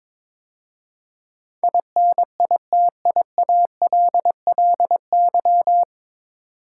Morse code, heard as a keyed tone, about 23 wpm, 700 Hz. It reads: INITIALLY
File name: 9 Letter Words - First 3 Letters Given - 22wpm